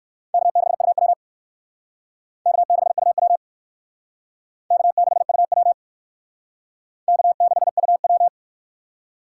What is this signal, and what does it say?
Morse code, heard as a keyed tone, about 56 wpm, 700 Hz. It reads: K6UW K6UW K6UW K6UW
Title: Speed Racing - USA Call Signs 45wpm